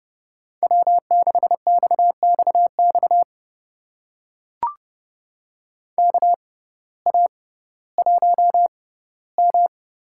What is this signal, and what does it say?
Morse code, heard as a keyed tone, about 30 wpm, spaced as for 17 wpm, 700 Hz. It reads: W6XXX KA1M
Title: USA Call Signs - 6x Character Spacing - 30wpm